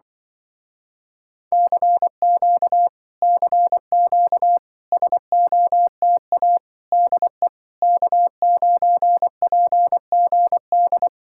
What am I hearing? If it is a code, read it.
Morse code, heard as a keyed tone, about 25 wpm, 700 Hz. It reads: CQ CQ SOTA DE K9PGD
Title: Speed-Racing Common QSO Phrases - 20wpm